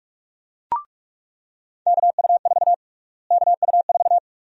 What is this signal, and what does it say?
Morse code, heard as a keyed tone, about 42 wpm, 700 Hz. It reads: KU4 KU4
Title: USA Call Sign Prefixes - Repeated 2x - 45wpm